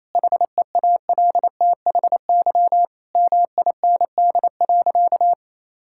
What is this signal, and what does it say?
Morse code, heard as a keyed tone, about 28 wpm, 700 Hz. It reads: HEALTHY MIND.
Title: Sentences from Top 1000 Words - Review Part 5 - 28wpm